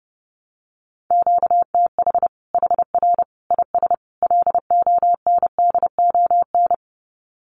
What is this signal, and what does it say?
Morse code, heard as a keyed tone, about 30 wpm, 700 Hz. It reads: QTH HR IS LONDON
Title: Common QSO Phrases 30wpm